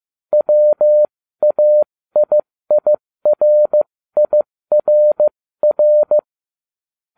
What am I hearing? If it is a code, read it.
Morse code, heard as a keyed tone, about 15 wpm, spaced as for 13 wpm, 600 Hz. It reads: WAIIRIRR